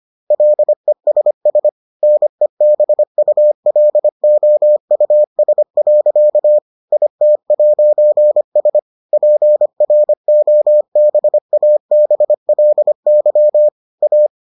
Morse code, heard as a keyed tone, about 25 wpm, 600 Hz. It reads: LESS NEBULOUS. IT'S PROBABLY A